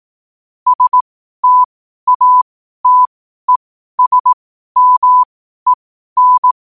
Morse code, heard as a keyed tone, about 18 wpm, spaced as for 12 wpm, 1000 Hz. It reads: STATESMEN
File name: qotdfiles / 12wpm